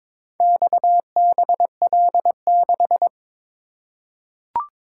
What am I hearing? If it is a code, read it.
Morse code, heard as a keyed tone, about 23 wpm, 700 Hz. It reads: XBL6
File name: practice / 22wpm / Four Letter-Number Combinations - 22wpm